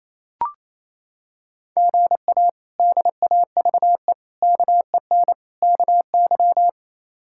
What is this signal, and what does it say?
Morse code, heard as a keyed tone, about 28 wpm, 700 Hz. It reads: GA DAVE KEN KY
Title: K1USN SST QSO Practice - 28wpm